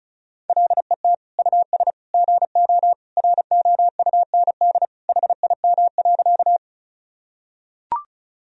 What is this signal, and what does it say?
Morse code, heard as a keyed tone, about 34 wpm, 700 Hz. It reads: LET US GO ROUND HIM.
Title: Sentences from Top 1000 Words - Review Part 1 - 35wpm